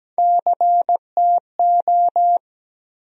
Morse code, heard as a keyed tone, about 17 wpm, 700 Hz. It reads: CTO
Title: USA Call Sign Postfixes - 3 Letters 17wpm